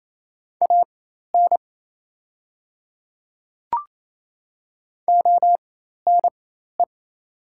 Morse code, heard as a keyed tone, about 28 wpm, spaced as for 13 wpm, 700 Hz. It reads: AN ONE
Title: 4x Character Spacing - Top 100 Words - 28wpm